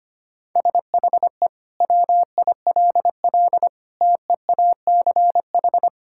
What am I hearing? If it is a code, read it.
Morse code, heard as a keyed tone, about 25 wpm, 700 Hz. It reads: SHE WILL TEACH
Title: Sentences from Top 1000 Words - Review Part 6 - 25wpm